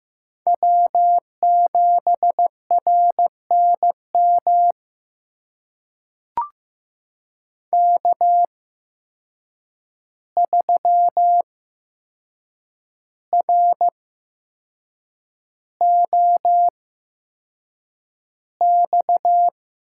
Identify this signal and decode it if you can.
Morse code, heard as a keyed tone, about 15 wpm, spaced as for 6 wpm, 700 Hz. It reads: W7RNM K3ROX
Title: USA Call Signs - 5 Characters - 8x Character Spacing - 15wpm